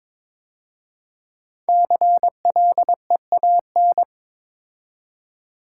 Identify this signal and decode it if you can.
Morse code, heard as a keyed tone, about 22 wpm, 700 Hz. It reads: CLEAN
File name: 5 Letter Words - First 3 Letters Given - 22wpm